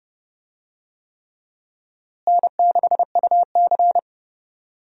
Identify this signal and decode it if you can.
Morse code, heard as a keyed tone, about 30 wpm, 700 Hz. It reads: N6UC
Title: USA Call Signs - 4 Characters - Repeated 3x - 30wpm